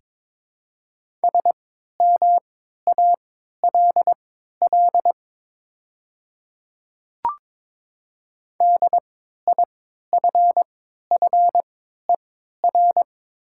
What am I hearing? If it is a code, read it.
Morse code, heard as a keyed tone, about 22 wpm, spaced as for 12 wpm, 700 Hz. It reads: SMALL DIFFER
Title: 3x Character Spacing - Top 300 Words - 22wpm